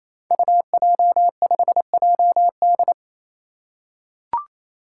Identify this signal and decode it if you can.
Morse code, heard as a keyed tone, about 29 wpm, 700 Hz. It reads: UJ5JD
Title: Code Groups - 5 Letter:Numbers - 28wpm